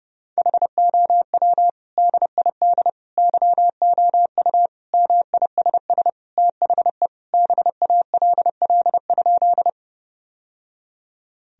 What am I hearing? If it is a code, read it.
Morse code, heard as a keyed tone, about 30 wpm, 700 Hz. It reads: HOW DID YOU MISS THE BALL?
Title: Sentences from Top 1000 Words - Review Part 3 - 30wpm